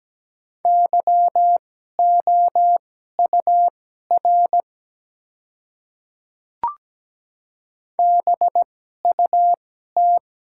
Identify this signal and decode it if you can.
Morse code, heard as a keyed tone, about 17 wpm, spaced as for 12 wpm, 700 Hz. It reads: YOUR BUT
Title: 2x Character Spacing - Top 100 Words - 17wpm